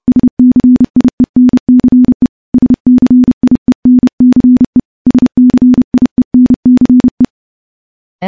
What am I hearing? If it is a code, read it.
Morse code, heard as a keyed tone, about 30 wpm, 250 Hz. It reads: SCIENCE SCIENCE SCIENCE
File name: morse / morse-training-30WPM